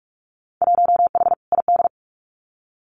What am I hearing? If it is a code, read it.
Morse code, heard as a keyed tone, about 44 wpm, 700 Hz. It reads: 1H ID
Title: Winter Field Day Exchanges 45wpm